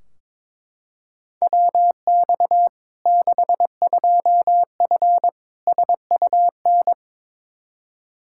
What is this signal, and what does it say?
Morse code, heard as a keyed tone, about 22 wpm, 700 Hz. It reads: WX 62F SUN
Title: Practice QSO - Exchange 2 - Info - 22wpm